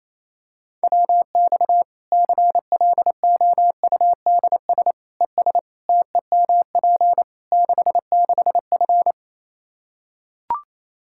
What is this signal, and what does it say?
Morse code, heard as a keyed tone, about 28 wpm, 700 Hz. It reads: WX CLOUDS ES TEMP 66F
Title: Practice QSO - Exchange 2 - Info - 28wpm